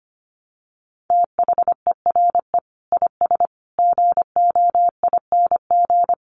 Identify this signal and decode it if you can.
Morse code, heard as a keyed tone, about 25 wpm, 700 Hz. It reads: THERE IS GOING
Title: Sets of 5 Words Encore 25wpm